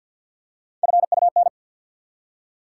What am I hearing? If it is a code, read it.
Morse code, heard as a keyed tone, about 50 wpm, 700 Hz. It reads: FUN
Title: Three Letter Words - 50wpm